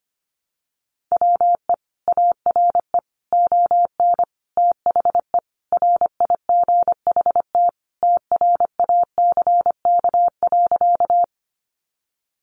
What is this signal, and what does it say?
Morse code, heard as a keyed tone, about 25 wpm, 700 Hz. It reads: WE ARE ON THE RIGHT TRACK.
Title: Sentences from Top 1100 Words 25wpm